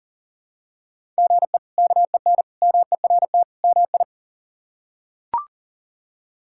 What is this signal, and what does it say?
Morse code, heard as a keyed tone, about 40 wpm, 700 Hz. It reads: GE KEN MERT MI
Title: K1USN SST QSO Practice - 40wpm